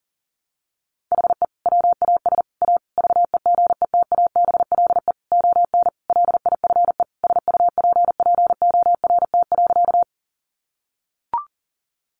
Morse code, heard as a keyed tone, about 40 wpm, 700 Hz. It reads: HE WAS A VEGETABLE ON LIFE SUPPORT.
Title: Sentences from Top 1300 Words 40wpm